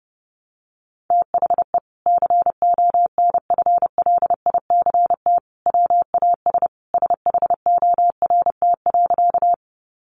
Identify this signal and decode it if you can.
Morse code, heard as a keyed tone, about 30 wpm, 700 Hz. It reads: THE CONFLICT WAS SHORT.
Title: Sentences from Top 1300 Words 30wpm